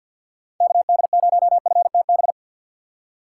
Morse code, heard as a keyed tone, about 65 wpm, spaced as for 57 wpm, 700 Hz. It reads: KD0FTB
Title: Speed Racing - USA 2x3 Call Signs 50wpm